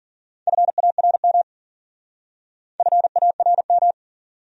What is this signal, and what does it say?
Morse code, heard as a keyed tone, about 47 wpm, 700 Hz. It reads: FARM FARM
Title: Speed-Racing Top 500 Words 40wpm